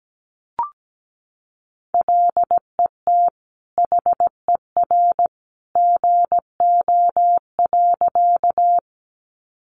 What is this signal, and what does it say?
Morse code, heard as a keyed tone, about 17 wpm, 700 Hz. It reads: LET HER GO.
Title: Sentences from Top 300 Words 17wpm